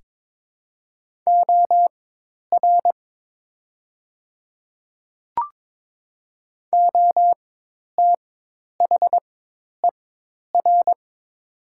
Morse code, heard as a keyed tone, about 22 wpm, spaced as for 10 wpm, 700 Hz. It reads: OR OTHER
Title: 4x Character Spacing - Top 100 Words - 22wpm